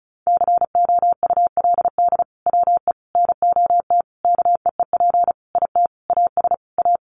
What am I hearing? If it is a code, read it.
Morse code, heard as a keyed tone, about 34 wpm, 700 Hz. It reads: COULD WE NOT KEEP IT AS A
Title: Sentences from Top 1200 Words 35wpm